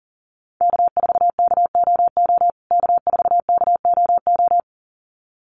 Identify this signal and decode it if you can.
Morse code, heard as a keyed tone, about 40 wpm, 700 Hz. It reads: K4KOO K4KOO
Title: USA Call Signs Repeated 2x 40wpm